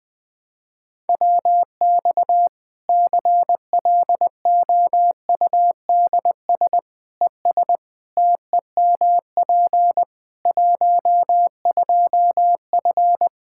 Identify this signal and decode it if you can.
Morse code, heard as a keyed tone, about 20 wpm, 700 Hz. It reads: WX CLOUDS ES TEMP 12F
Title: Practice QSO - Exchange 2 - Info - 20wpm